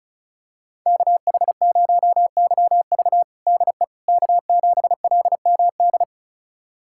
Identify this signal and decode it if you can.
Morse code, heard as a keyed tone, about 36 wpm, 700 Hz. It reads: KH0YV DE K7LMD
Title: USA Callsign DE Callsign - 5 Characters - 35wpm